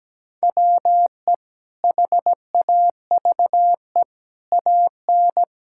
Morse code, heard as a keyed tone, about 17 wpm, 700 Hz. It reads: WE HAVE AN
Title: Sentences from Top 1000 Words - Review Part 6 - 17wpm